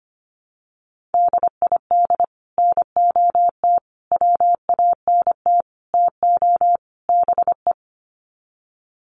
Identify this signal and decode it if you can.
Morse code, heard as a keyed tone, about 25 wpm, 700 Hz. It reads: DID NOT WANT TO BE
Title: Sets of 5 Words 25wpm